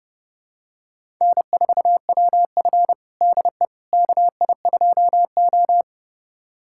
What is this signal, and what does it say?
Morse code, heard as a keyed tone, about 32 wpm, 700 Hz. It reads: N4WF DE KI2O
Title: USA Callsign DE Callsign - 4 Characters - 30wpm